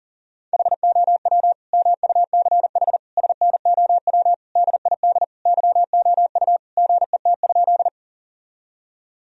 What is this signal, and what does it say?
Morse code, heard as a keyed tone, about 40 wpm, 700 Hz. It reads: HOW MUCH SNOW DID YOU GET?
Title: Sentences from Top 1000 Words - Review Part 3 - 40wpm